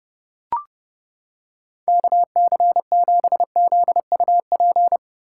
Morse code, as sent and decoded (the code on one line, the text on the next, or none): -.- -.-. --... --.. ..- .--.
KC7ZUP